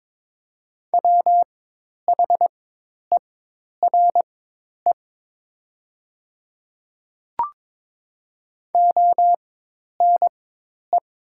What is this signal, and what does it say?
Morse code, heard as a keyed tone, about 21 wpm, spaced as for 10 wpm, 700 Hz. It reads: WHERE ONE
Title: 4x Character Spacing - Top 200 Words - 22wpm